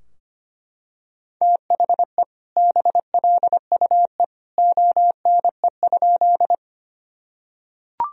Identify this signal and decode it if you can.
Morse code, heard as a keyed tone, about 25 wpm, 700 Hz. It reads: THE BLUE ONE?
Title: Sentences from Top 1000 Words - Review Part 3 - 25wpm